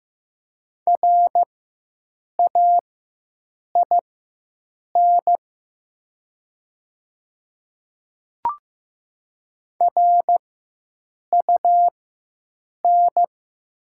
Morse code, heard as a keyed tone, about 15 wpm, spaced as for 7 wpm, 700 Hz. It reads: RAIN RUN
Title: 4x Character Spacing - Top 300 Words - 15wpm